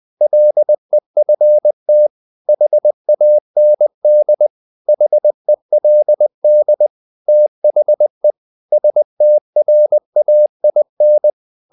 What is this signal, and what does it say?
Morse code, heard as a keyed tone, about 20 wpm, 600 Hz. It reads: LEFT HAND HELD THE STRAIN
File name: the old man and the sea part 2 20wpm